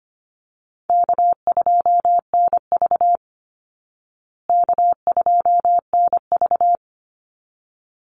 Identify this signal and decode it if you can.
Morse code, heard as a keyed tone, about 25 wpm, 700 Hz. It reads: K2NV K2NV
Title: Call Signs Repeated 3x - 25wpm